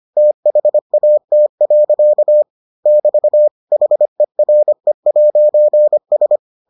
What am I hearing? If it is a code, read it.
Morse code, heard as a keyed tone, about 25 wpm, 600 Hz. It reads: THAT. <BT> HERE'S